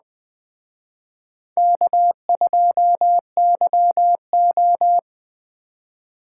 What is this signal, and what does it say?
Morse code, heard as a keyed tone, about 20 wpm, 700 Hz. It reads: K2YO
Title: Speed Racing - USA 1x2 Call Signs 17wpm